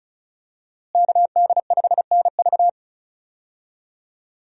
Morse code, heard as a keyed tone, about 34 wpm, 700 Hz. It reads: KD5NV
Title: USA Call Signs - 5 Characters - Repeated 3x - 35wpm